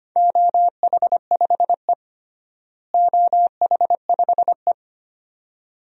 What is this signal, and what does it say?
Morse code, heard as a keyed tone, about 25 wpm, 700 Hz. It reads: OH5E OH5E
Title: DXCC-100 Call Signs - 4 Characters - Repeated 2x - 25wpm